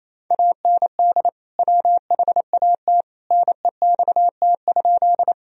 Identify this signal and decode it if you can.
Morse code, heard as a keyed tone, about 28 wpm, 700 Hz. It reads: AND WHAT NEXT?
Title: Sentences from Top 1000 Words - Review Part 1 - 28wpm